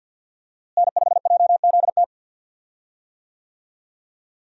Morse code, heard as a keyed tone, about 48 wpm, 700 Hz. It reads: N5JZT
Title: USA Call Signs - 5 Characters - Repeated 2x - 50wpm